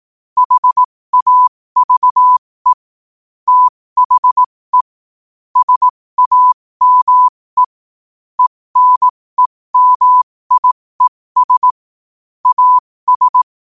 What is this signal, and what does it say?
Morse code, heard as a keyed tone, about 18 wpm, spaced as for 15 wpm, 1000 Hz. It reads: HAVE THE SAME ENEMIES AS
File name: 15wpm